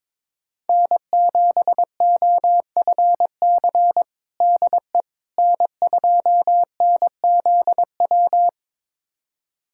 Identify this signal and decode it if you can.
Morse code, heard as a keyed tone, about 21 wpm, 700 Hz. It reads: N7OFC DE N2NZW
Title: USA Callsign DE Callsign - 5 Characters - 22wpm